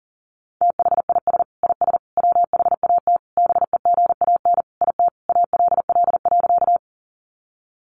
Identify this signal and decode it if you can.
Morse code, heard as a keyed tone, about 37 wpm, 700 Hz. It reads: THIS IS WHAT BEGAN IT ALL.
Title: Sentences from Top 300 Words 40wpm